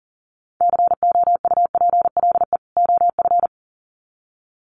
Morse code, heard as a keyed tone, about 40 wpm, 700 Hz. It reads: COUPLE OF
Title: Sets of 2 Words 40wpm